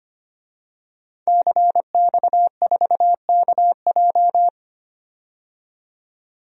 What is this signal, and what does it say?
Morse code, heard as a keyed tone, about 25 wpm, 700 Hz. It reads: CX4KJ
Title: DXCC-100 Call Signs - 5 Characters - 25wpm